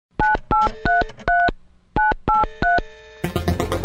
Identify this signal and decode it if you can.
DTMF, heard as telephone touch tones, tones of about 160 ms. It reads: 9833 983